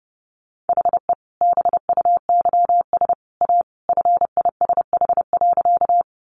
Morse code, heard as a keyed tone, about 30 wpm, 700 Hz. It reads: HE BUYS A FISH.
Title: Additional Easy Phrases 30wpm